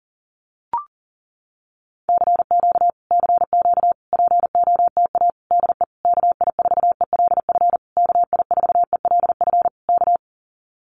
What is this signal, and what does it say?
Morse code, heard as a keyed tone, about 40 wpm, 700 Hz. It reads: CQ CQ POTA DE KI4ELF KI4ELF K
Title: Common QSO Phrases Encore 40wpm